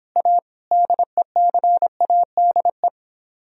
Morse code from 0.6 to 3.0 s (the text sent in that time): DECADE